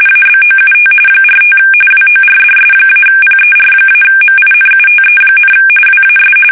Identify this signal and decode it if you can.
RTTY, about 45.45 baud. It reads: ABOUT 45 WORDS-PER-MINUTE, WAS
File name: fsk850 / teletype-corp